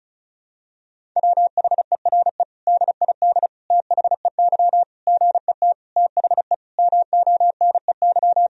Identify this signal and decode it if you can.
Morse code, heard as a keyed tone, about 34 wpm, 700 Hz. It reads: WHERE DID THEY GET THE MONEY
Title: Sentences from Top 400 Words and Abbreviations 35wpm